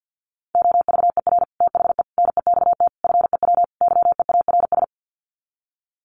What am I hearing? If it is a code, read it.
Morse code, heard as a keyed tone, about 50 wpm, 700 Hz. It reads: OVER THE NEXT FEW YEARS
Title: Sets of 5 Words Encore 50wpm